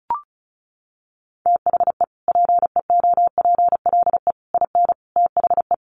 Morse code, heard as a keyed tone, about 34 wpm, 700 Hz. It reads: THE PEOPLE IN THE
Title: Sentences from Top 1400 Words 35wpm